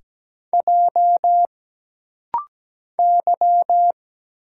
Morse code, heard as a keyed tone, about 17 wpm, 700 Hz. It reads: J Y
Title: Single Letters - Rapid-Fire 17wpm